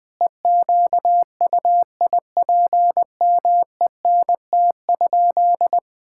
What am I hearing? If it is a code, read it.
Morse code, heard as a keyed tone, about 20 wpm, 700 Hz. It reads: EQUIPMENT?
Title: Sentences from Top 1000 Words - Review Part 5 - 20wpm